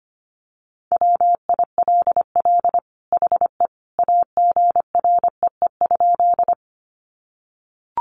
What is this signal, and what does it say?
Morse code, heard as a keyed tone, about 25 wpm, 700 Hz. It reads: WILL HE AGREE?
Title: Sentences from Top 1100 Words 25wpm